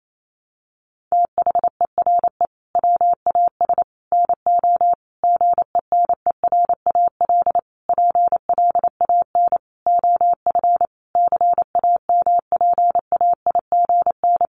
Morse code, heard as a keyed tone, about 28 wpm, 700 Hz. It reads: THERE WAS NO GENERAL PLAN OF CAMPAIGN
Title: Sentences from Top 1200 Words 28wpm